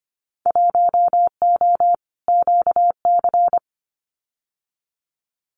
Morse code, heard as a keyed tone, about 25 wpm, 700 Hz. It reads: 1O QC
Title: Winter Field Day Exchanges 25wpm